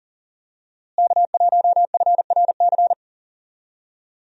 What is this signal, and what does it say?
Morse code, heard as a keyed tone, about 37 wpm, 700 Hz. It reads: K1FRC
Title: USA Call Signs - 5 Characters - Repeated 2x - 40wpm